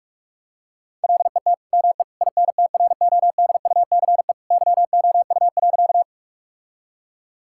Morse code, heard as a keyed tone, about 46 wpm, 700 Hz. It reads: LET ME INTRODUCE YOU.
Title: Sentences from Top 1600 Words 45wpm